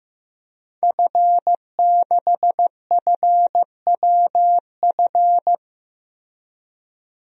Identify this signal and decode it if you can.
Morse code, heard as a keyed tone, about 15 wpm, 700 Hz. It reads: F6FWF
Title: DXCC-100 Call Signs - 5 Characters - Repeated 2x - 15wpm